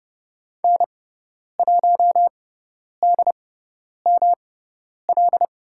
Morse code, heard as a keyed tone, about 30 wpm, spaced as for 10 wpm, 700 Hz. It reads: N1DML
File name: Lesson 18g - Letter m - Call Signs 30wpm